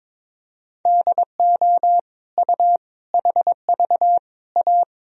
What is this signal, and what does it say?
Morse code, heard as a keyed tone, about 22 wpm, 700 Hz. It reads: DO U HV A
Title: Sentences from Top 300 Words and Abbreviations 22wpm